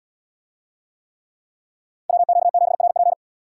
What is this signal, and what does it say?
Morse code, heard as a keyed tone, about 69 wpm, 700 Hz. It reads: WQ7RP